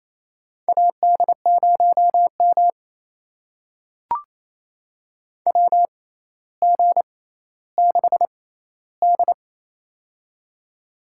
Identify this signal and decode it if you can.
Morse code, heard as a keyed tone, about 27 wpm, spaced as for 14 wpm, 700 Hz. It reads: AD0M WG6D
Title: USA Call Signs - 4 Characters - 6x Character Spacing - 28wpm